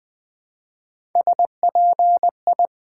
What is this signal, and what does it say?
Morse code, heard as a keyed tone, about 20 wpm, 700 Hz. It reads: SPI
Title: Build A Word - 1 Letter At A Time - 3 to 5-Letter Words - 20wpm